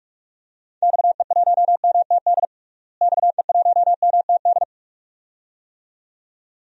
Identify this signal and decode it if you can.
Morse code, heard as a keyed tone, about 44 wpm, 700 Hz. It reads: XE1MTD XE1MTD
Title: DXCC-100 Call Signs - 6 Characters - Repeated 2x - 45wpm